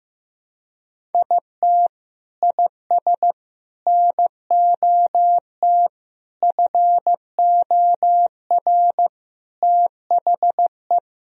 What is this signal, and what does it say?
Morse code, heard as a keyed tone, about 15 wpm, 700 Hz. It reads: IT IS NOT FOR THE
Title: Sentences from Top 1000 Words - Review Part 1 - 15wpm